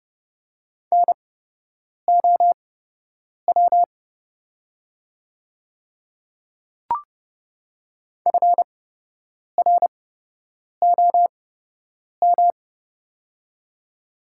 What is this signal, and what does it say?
Morse code, heard as a keyed tone, about 30 wpm, spaced as for 8 wpm, 700 Hz. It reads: NOW FROM
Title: 8x Character Spacing - Top 100 Words - 30wpm